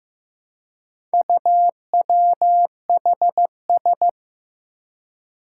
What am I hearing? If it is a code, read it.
Morse code, heard as a keyed tone, about 16 wpm, 700 Hz. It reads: UWHS
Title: Four Challenging Letter-Number Combinations 15wpm